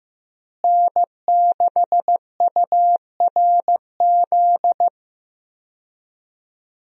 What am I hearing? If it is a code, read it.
Morse code, heard as a keyed tone, about 15 wpm, 700 Hz. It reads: N6URZ
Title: USA Call Signs - 5 Characters - Repeated 2x - 15wpm